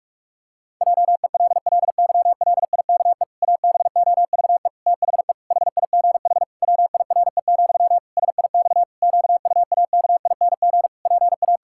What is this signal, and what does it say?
Morse code, heard as a keyed tone, about 46 wpm, 700 Hz. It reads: JELLYLIKE ABOVE THE HIGH WIRE, SIX QUAKING PA
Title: Pangram Challenge v2 - 45wpm